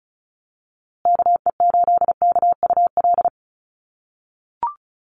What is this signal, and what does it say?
Morse code, heard as a keyed tone, about 34 wpm, 700 Hz. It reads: KE8KUL
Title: USA Call Signs Repeated 3x 35wpm